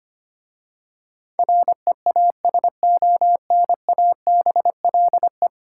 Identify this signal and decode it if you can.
Morse code, heard as a keyed tone, about 25 wpm, 700 Hz. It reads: REASONABLE
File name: 10 Letter Words - First 3 Letters Given - 25wpm